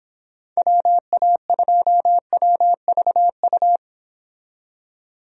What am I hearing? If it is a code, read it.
Morse code, heard as a keyed tone, about 29 wpm, 700 Hz. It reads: WA2WVU